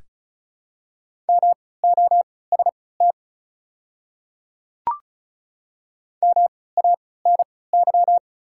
Morse code, heard as a keyed tone, about 36 wpm, spaced as for 20 wpm, 690 Hz. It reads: MOST MANY
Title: 3x Character Spacing - Top 100 Words - 35wpm